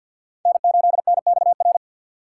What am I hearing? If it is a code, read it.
Morse code, heard as a keyed tone, about 53 wpm, 700 Hz. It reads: N8NXR
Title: USA Call Signs - 5 Characters - 4x Character Spacing - 50wpm